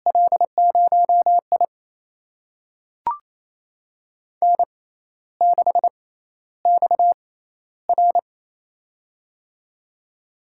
Morse code, heard as a keyed tone, about 28 wpm, spaced as for 13 wpm, 700 Hz. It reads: L0I N6XR
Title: USA Call Signs - 4 Characters - 6x Character Spacing - 28wpm